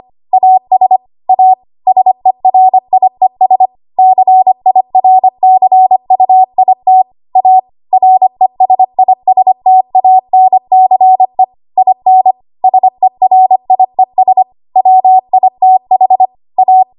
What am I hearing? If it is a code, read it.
Morse code, heard as a keyed tone, about 25 wpm, 750 Hz. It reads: AS A SERIES CIRCUIT A RESISTANCE IN SERIES WITH A
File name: Archive / 25 WPM / 25WPM